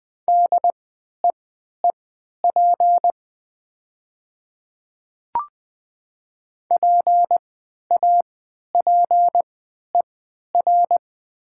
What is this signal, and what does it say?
Morse code, heard as a keyed tone, about 20 wpm, spaced as for 11 wpm, 700 Hz. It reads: DEEP PAPER
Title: 3x Character Spacing - Top 500 Words - 20wpm